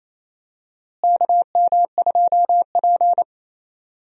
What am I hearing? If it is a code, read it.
Morse code, heard as a keyed tone, about 28 wpm, 700 Hz. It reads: KM2P
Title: USA Call Signs - 4 Characters - Repeated 3x - 28wpm